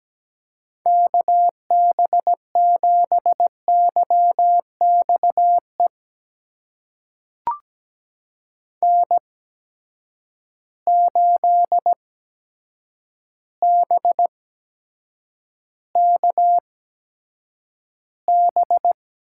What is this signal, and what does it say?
Morse code, heard as a keyed tone, about 17 wpm, spaced as for 7 wpm, 700 Hz. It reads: KB7YX E N8BKB
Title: USA Call Signs - 8x Character Spacing - 17wpm